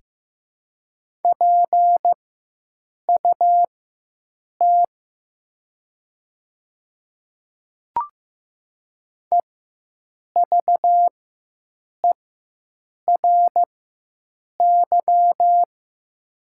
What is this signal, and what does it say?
Morse code, heard as a keyed tone, about 15 wpm, spaced as for 7 wpm, 700 Hz. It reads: PUT EVERY
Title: 4x Character Spacing - Top 200 Words - 15wpm